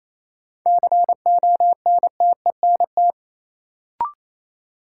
Morse code, heard as a keyed tone, about 28 wpm, 700 Hz. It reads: CONTENT
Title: Sentences from Top 900 Words - New Word Drill - 28wpm